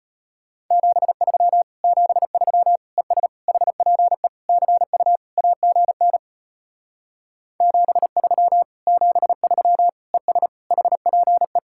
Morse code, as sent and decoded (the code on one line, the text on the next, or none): --... ...-- / --... ...-- / . ... / .... .--. / -.-. ..- / .- --. -. / --... ...-- / --... ...-- / . ... / .... .--. .
73 73 ES HPE CU AGN 73 73 ES HPE